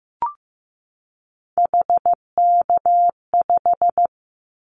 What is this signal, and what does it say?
Morse code, heard as a keyed tone, about 15 wpm, 700 Hz. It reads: HK5
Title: Callsign Prefixes for Top 100 DXCC - 15wpm